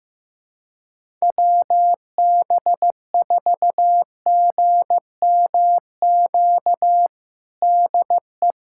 Morse code, heard as a keyed tone, about 16 wpm, 700 Hz. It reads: WB4GMQ DE
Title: USA Callsign DE Callsign - 6 Characters - 15wpm